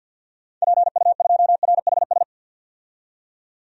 Morse code, heard as a keyed tone, about 47 wpm, 700 Hz. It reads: PU2RHS
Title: DXCC-100 Call Signs - 6 Characters - 50wpm